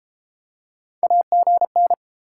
Morse code, heard as a keyed tone, about 30 wpm, 700 Hz. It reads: AGN